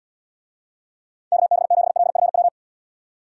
Morse code, heard as a keyed tone, about 79 wpm, 700 Hz. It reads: KB7LFW